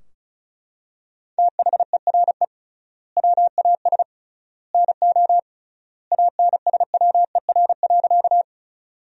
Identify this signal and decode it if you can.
Morse code, heard as a keyed tone, about 36 wpm, 700 Hz. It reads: THERE WAS NO ANSWER.
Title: 3x Word-Spacing - Sentences from Top 300 Words - 35wpm